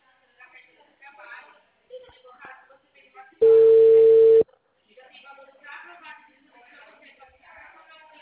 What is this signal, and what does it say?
Ringback tone (China, 450 Hz), with speech.